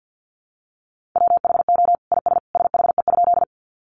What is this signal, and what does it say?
Morse code, heard as a keyed tone, about 50 wpm, 700 Hz. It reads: WHO IS SHE?
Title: Sentences from Top 100 Words 50wpm